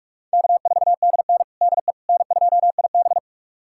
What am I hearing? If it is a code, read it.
Morse code, heard as a keyed tone, about 46 wpm, 700 Hz. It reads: K4DN DE N2IB
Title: USA Callsign DE Callsign - 4 Characters - 45wpm